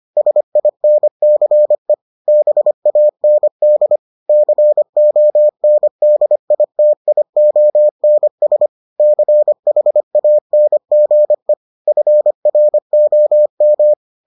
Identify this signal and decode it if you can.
Morse code, heard as a keyed tone, about 25 wpm, 600 Hz. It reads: SINCE BAND CONDITIONS CHANGE FROM